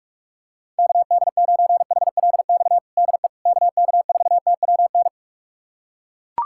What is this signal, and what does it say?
Morse code, heard as a keyed tone, about 41 wpm, 690 Hz. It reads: KD9HLX DE KK4TWN
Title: USA Callsign DE Callsign - 6 Characters - 45wpm